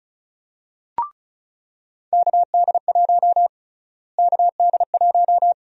Morse code, heard as a keyed tone, about 34 wpm, 700 Hz. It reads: KD1 KD1
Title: USA Call Sign Prefixes - Repeated 2x - 35wpm